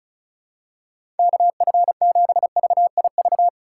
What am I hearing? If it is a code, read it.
Morse code, heard as a keyed tone, about 36 wpm, 700 Hz. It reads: KF7VIV
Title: USA Call Signs - 6 Characters - 8x Character Spacing - 35wpm